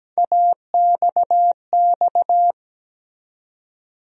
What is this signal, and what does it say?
Morse code, heard as a keyed tone, about 17 wpm, 700 Hz. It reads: AXX
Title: USA Call Sign Postfixes - 3 Letters 17wpm